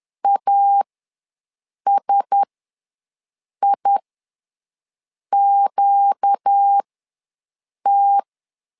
Morse code, heard as a keyed tone, about 10 wpm, spaced as for 5 wpm, 800 Hz. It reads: ASIQT
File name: LEZIONI / 4 Q1S ws 10wpm 2208